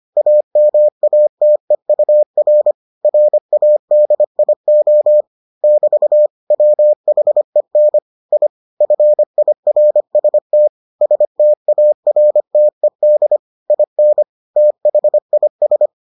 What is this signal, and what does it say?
Morse code, heard as a keyed tone, about 25 wpm, 600 Hz. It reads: AMATEUR RADIO <BT> WHEN I FIRST STARTED IN THIS